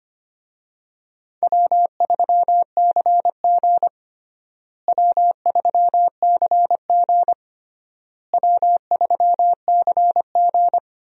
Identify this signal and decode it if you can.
Morse code, heard as a keyed tone, about 25 wpm, 700 Hz. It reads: W3CG W3CG W3CG
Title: USA Call Signs - 4 Characters - Repeated 3x - 25wpm